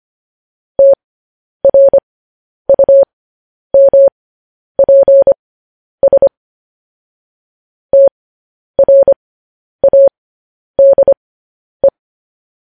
Morse code, heard as a keyed tone, about 25 wpm, spaced as for 10 wpm, 550 Hz. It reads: TRUMPS TRADE
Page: news headlines in morse code at 10 wpm